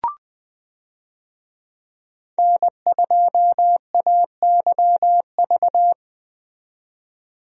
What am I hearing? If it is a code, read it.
Morse code, heard as a keyed tone, about 20 wpm, 700 Hz. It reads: N2AYV
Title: USA Call Signs - 5 Characters - 20wpm